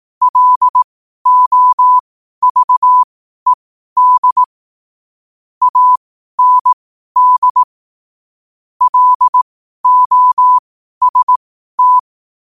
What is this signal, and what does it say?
Morse code, heard as a keyed tone, about 18 wpm, spaced as for 12 wpm, 1000 Hz. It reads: LOVED AND LOST
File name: qotdfiles / 12wpm